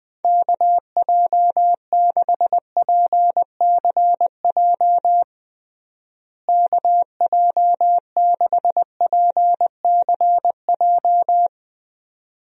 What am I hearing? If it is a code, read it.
Morse code, heard as a keyed tone, about 20 wpm, 700 Hz. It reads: KJ6PCJ KJ6PCJ
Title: USA Call Signs - 6 Characters - Repeated 2x - 20wpm